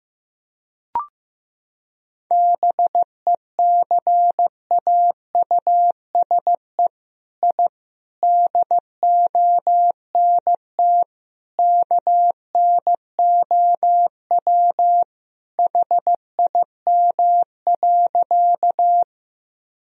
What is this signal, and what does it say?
Morse code, heard as a keyed tone, about 15 wpm, 700 Hz. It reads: BECAUSE I DONT KNOW HIM.
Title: Sentences from Top 1000 Words - Review Part 3 - 15wpm